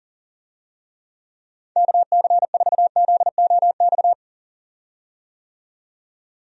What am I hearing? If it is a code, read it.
Morse code, heard as a keyed tone, about 40 wpm, 700 Hz. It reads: KC4ZOX
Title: USA Call Signs - 6 Characters - 40wpm